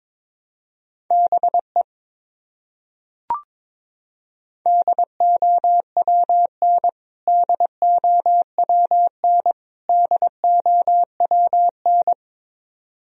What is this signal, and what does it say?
Morse code, heard as a keyed tone, about 22 wpm, 700 Hz. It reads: BE DOWN DOWN DOWN
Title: Top 100 Words Repeated 22wpm